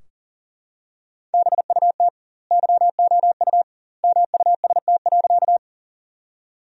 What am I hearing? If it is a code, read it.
Morse code, heard as a keyed tone, about 40 wpm, 700 Hz. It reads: BUT YOU MUST.
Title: 2x Word-Spacing - Sentences from Top 200 Words - 40wpm